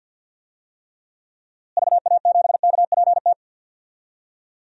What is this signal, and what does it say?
Morse code, heard as a keyed tone, about 47 wpm, 700 Hz. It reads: VA7KPT
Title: DXCC-100 Call Signs - 6 Characters - Repeated 2x - 50wpm